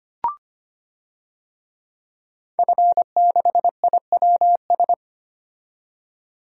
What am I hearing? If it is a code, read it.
Morse code, heard as a keyed tone, about 25 wpm, 700 Hz. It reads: F6IWS
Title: DXCC-100 Call Signs - 5 Characters - 25wpm